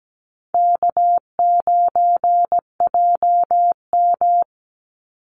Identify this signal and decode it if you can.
Morse code, heard as a keyed tone, about 17 wpm, 700 Hz. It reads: K9JM
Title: Easy Call Signs - 17wpm